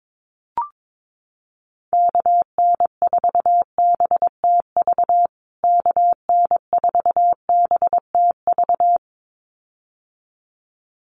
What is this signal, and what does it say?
Morse code, heard as a keyed tone, about 22 wpm, 700 Hz. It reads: KN4BTV KN4BTV
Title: USA Call Signs Repeated 2x 22wpm